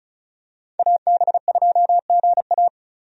Morse code, heard as a keyed tone, about 36 wpm, 700 Hz. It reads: AB2GA
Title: USA Call Signs - 5 Characters - 35wpm